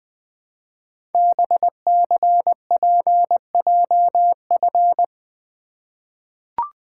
Morse code, heard as a keyed tone, about 20 wpm, 700 Hz. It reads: BCPJF
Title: Code Groups - 5 Letters - 20wpm